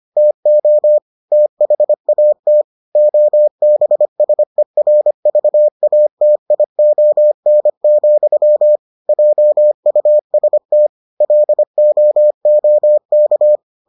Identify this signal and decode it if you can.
Morse code, heard as a keyed tone, about 25 wpm, 600 Hz. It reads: TO THAT OBSERVATION, JUST LOOK